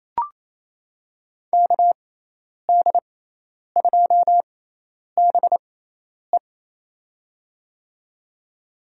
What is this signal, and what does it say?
Morse code, heard as a keyed tone, about 28 wpm, spaced as for 10 wpm, 700 Hz. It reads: KD2BE
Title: USA Call Signs - 5 Characters - 6x Character Spacing - 28wpm